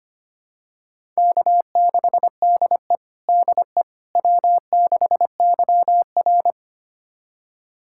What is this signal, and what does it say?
Morse code, heard as a keyed tone, about 25 wpm, 700 Hz. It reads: K6DE DE W6YR
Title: USA Callsign DE Callsign - 4 Characters - 25wpm